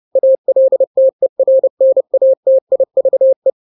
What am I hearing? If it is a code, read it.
Morse code, heard as a keyed tone, about 30 wpm, 530 Hz. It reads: ALTERNATIVE
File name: daily / 29wpm